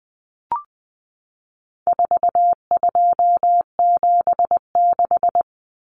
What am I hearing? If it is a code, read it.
Morse code, heard as a keyed tone, about 20 wpm, 700 Hz. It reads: 4276
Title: Common Cut Numbers - 4-Digit 20wpm